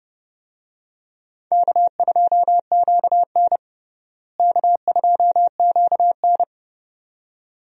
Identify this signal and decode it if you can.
Morse code, heard as a keyed tone, about 30 wpm, 700 Hz. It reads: K2QN K2QN
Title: USA Call Signs - 4 Characters - Repeated 2x - 30wpm